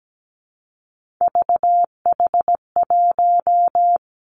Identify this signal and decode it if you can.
Morse code, heard as a keyed tone, about 17 wpm, 700 Hz. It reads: VH1
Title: Three Challenging Letter-Number Combinations 17wpm